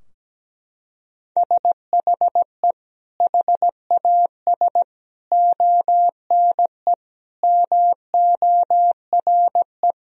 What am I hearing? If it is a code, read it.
Morse code, heard as a keyed tone, about 17 wpm, 700 Hz. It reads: SHE HAS ONE MORE
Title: Sentences from Top 1000 Words - Review Part 4 - 17wpm